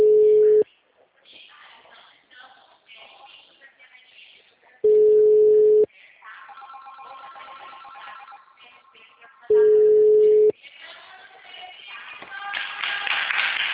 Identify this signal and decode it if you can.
Ringback tone (Europe, 425 Hz), with speech.
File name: out- -2041- - - .57550